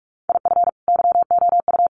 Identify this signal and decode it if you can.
Morse code, heard as a keyed tone, about 44 wpm, 700 Hz. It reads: IF YOU
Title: Sentences from Top 1100 Words 45wpm